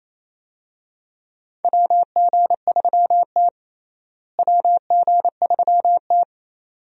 Morse code, heard as a keyed tone, about 28 wpm, 700 Hz. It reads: WG3T WG3T
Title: USA Call Signs - 4 Characters - Repeated 2x - 28wpm